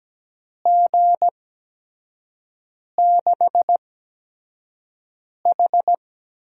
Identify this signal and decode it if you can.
Morse code, heard as a keyed tone, about 17 wpm, spaced as for 5 wpm, 700 Hz. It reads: G6H
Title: USA Call Signs - 6 Characters - 8x Character Spacing - 17wpm